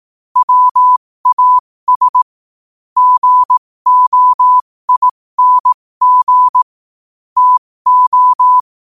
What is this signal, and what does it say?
Morse code, heard as a keyed tone, about 18 wpm, spaced as for 15 wpm, 1000 Hz. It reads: WAS GOING TO
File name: qotdfiles / 15wpm